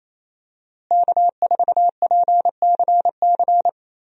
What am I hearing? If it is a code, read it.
Morse code, heard as a keyed tone, about 27 wpm, 700 Hz. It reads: K4PCC
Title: USA Call Signs - 5 Characters - Repeated 3x - 28wpm